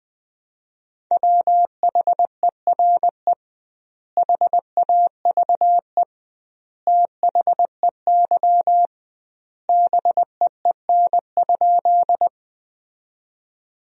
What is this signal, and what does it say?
Morse code, heard as a keyed tone, about 20 wpm, 700 Hz. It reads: WHERE HAVE THEY BEEN?
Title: 2x Word-Spacing - Sentences from Top 200 Words - 20wpm